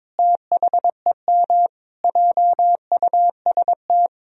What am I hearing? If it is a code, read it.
Morse code, heard as a keyed tone, about 21 wpm, 700 Hz. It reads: THEM JUST
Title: Sentences from Top 1000 Words - Review Part 2 - 22wpm